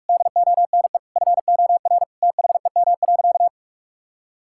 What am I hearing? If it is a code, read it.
Morse code, heard as a keyed tone, about 46 wpm, 700 Hz. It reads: DONE FOR THEM.
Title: Sentences from Top 1600 Words 45wpm